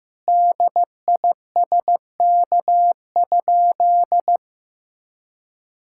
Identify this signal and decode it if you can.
Morse code, heard as a keyed tone, about 15 wpm, 700 Hz. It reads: DISK?
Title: Sentences from Top 1000 Words - Review Part 6 - 15wpm